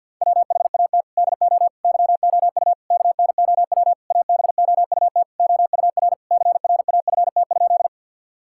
Morse code, heard as a keyed tone, about 50 wpm, 700 Hz. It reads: WHAT DO YOU KNOW ABOUT OUR CRAFT?
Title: Sentences from Top 1000 Words - Review Part 6 - 50wpm